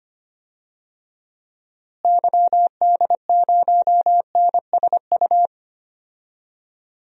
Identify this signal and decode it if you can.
Morse code, heard as a keyed tone, about 25 wpm, 700 Hz. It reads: YD0NSU
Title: DXCC-100 Call Signs - 6 Characters - Repeated 3x - 25wpm